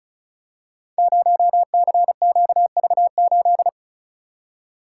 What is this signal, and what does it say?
Morse code, heard as a keyed tone, about 36 wpm, 700 Hz. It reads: 0CQV8
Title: Code Groups - 5 Letter:Numbers - 35wpm